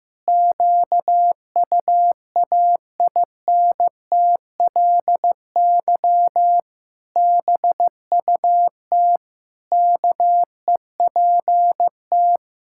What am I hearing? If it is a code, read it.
Morse code, heard as a keyed tone, about 15 wpm, 700 Hz. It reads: QUAINTLY BUT KEPT
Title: Pangram Challenge v2 - 15wpm